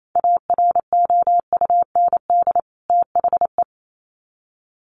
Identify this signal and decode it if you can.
Morse code, heard as a keyed tone, about 28 wpm, 700 Hz. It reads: AROUND THE
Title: Sets of 2 Words 28wpm